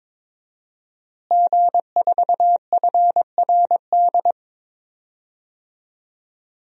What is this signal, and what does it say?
Morse code, heard as a keyed tone, about 22 wpm, 700 Hz. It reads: G4FRD
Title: DXCC-100 Call Signs - 5 Characters - Repeated 2x - 22wpm